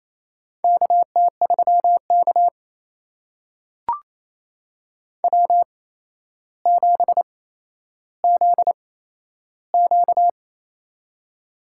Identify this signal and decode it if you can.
Morse code, heard as a keyed tone, about 28 wpm, spaced as for 12 wpm, 700 Hz. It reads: KT3K W7ZQ
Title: USA Call Signs - 4 Characters - 8x Character Spacing - 28wpm